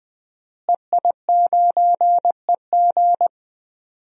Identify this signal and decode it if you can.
Morse code, heard as a keyed tone, about 20 wpm, 700 Hz. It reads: EI9EG
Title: DXCC-100 Call Signs - 5 Characters - Repeated 3x - 20wpm